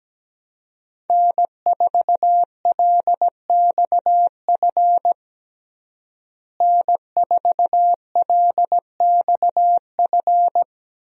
Morse code, heard as a keyed tone, about 17 wpm, 700 Hz. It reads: N4LXF N4LXF
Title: USA Call Signs - 5 Characters - Repeated 2x - 17wpm